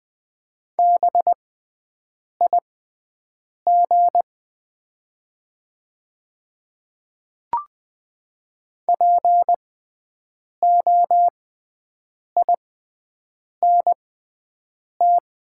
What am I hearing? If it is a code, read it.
Morse code, heard as a keyed tone, about 20 wpm, spaced as for 7 wpm, 700 Hz. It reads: BIG POINT
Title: 6x Character Spacing - Top 300 Words - 20wpm